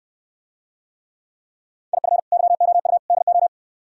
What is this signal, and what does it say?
Morse code, heard as a keyed tone, about 68 wpm, 700 Hz. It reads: IF YOU DO